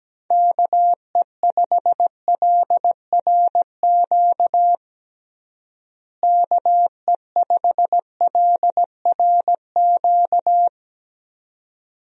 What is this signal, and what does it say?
Morse code, heard as a keyed tone, about 17 wpm, 700 Hz. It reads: KE5LRQ KE5LRQ
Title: USA Call Signs - 6 Characters - Repeated 2x - 17wpm